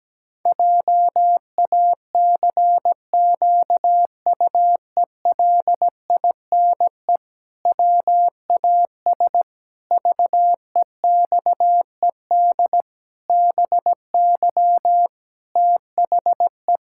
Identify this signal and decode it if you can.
Morse code, heard as a keyed tone, about 17 wpm, 700 Hz. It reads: JACQUELINE WAS VEXED BY THE
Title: Pangram Challenge v2 - 17wpm